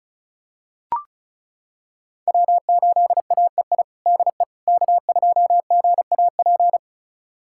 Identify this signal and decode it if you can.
Morse code, heard as a keyed tone, about 36 wpm, 700 Hz. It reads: W8AEI DE K2GAP
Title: USA Callsign DE Callsign - 5 Characters - 35wpm